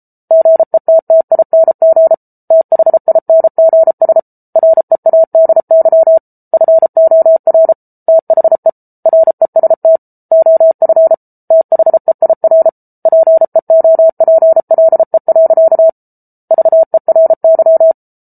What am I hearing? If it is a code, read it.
Morse code, heard as a keyed tone, about 34 wpm, 650 Hz. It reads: GETTING THINGS READY FOR THE REST OF THEIR PEOPLE. VERY